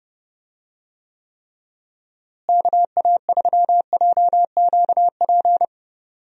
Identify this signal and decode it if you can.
Morse code, heard as a keyed tone, about 30 wpm, 700 Hz. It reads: KA3JQP
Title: USA Call Signs - 6 Characters - Repeated 3x - 30wpm